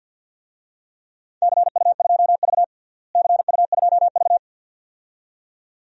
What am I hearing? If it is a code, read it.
Morse code, heard as a keyed tone, about 49 wpm, 700 Hz. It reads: CU2V CU2V
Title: DXCC-100 Call Signs - 4 Characters - Repeated 2x - 50wpm